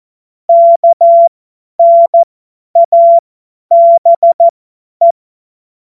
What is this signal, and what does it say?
Morse code, heard as a keyed tone, about 14 wpm, spaced as for 10 wpm, 680 Hz. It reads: KNABE
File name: Morsekurs / Witze / Witz 17. Mai 2022 50 F70